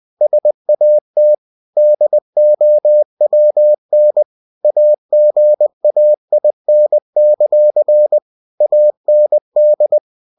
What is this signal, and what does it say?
Morse code, heard as a keyed tone, about 20 wpm, 600 Hz. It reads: SAT DOWN AGAIN; AND
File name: cw-books / winnie-the-pooh / winnie-20wpm-page-0036